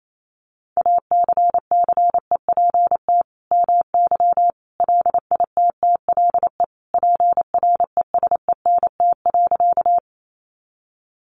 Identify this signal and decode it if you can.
Morse code, heard as a keyed tone, about 28 wpm, 700 Hz. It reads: ACCEPT MY LITTLE PRESENT.
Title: Sentences from Top 1300 Words 28wpm